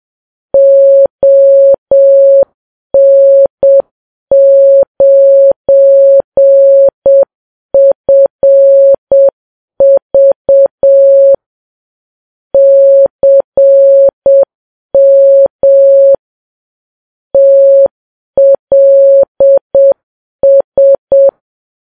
Morse code, heard as a keyed tone, about 7 wpm, 550 Hz. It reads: ON9FV CM TLS
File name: Files Web tls / Morse on9fv / morse7kps